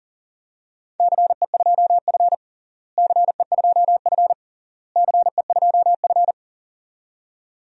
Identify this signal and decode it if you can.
Morse code, heard as a keyed tone, about 40 wpm, 700 Hz. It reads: CE2F CE2F CE2F
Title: DXCC-100 Call Signs - 4 Characters - Repeated 3x - 40wpm